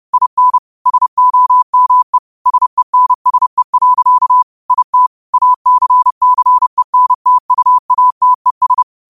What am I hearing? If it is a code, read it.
Morse code, heard as a keyed tone, about 30 wpm, 1000 Hz. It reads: IN SOME SENSE. IT ACCENTUATES
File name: qotdfiles / 30wpm